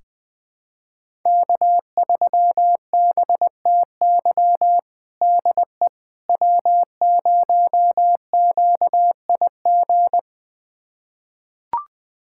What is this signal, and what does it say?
Morse code, heard as a keyed tone, about 20 wpm, 700 Hz. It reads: K3BTY DE W0QIG
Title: USA Callsign DE Callsign - 5 Characters - 20wpm